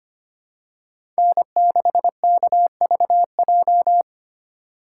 Morse code, heard as a keyed tone, about 25 wpm, 700 Hz. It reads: N6KVJ
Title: USA Call Signs - 5 Characters - 4x Character Spacing - 25wpm